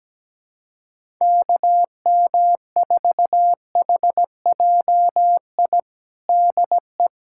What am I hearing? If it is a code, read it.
Morse code, heard as a keyed tone, about 17 wpm, 700 Hz. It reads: KM4HJI DE
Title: USA Callsign DE Callsign - 6 Characters - 17wpm